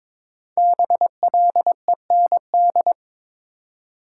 Morse code, heard as a keyed tone, about 22 wpm, 690 Hz. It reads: BLEND
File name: 5 Letter Words - First 3 Letters Given - 22wpm